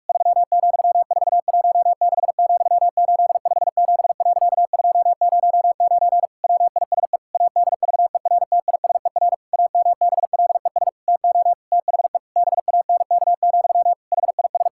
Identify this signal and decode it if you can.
Morse code, heard as a keyed tone, about 43 wpm, 700 Hz. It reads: 3,416,857.209 WISE ADVERTISER AMBLES TO THE BANK, HIS
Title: Pangram Challenge v2 - 45wpm